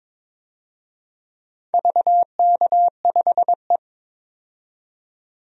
Morse code, heard as a keyed tone, about 21 wpm, 700 Hz. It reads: VK5E
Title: DXCC-100 Call Signs - 4 Characters - 22wpm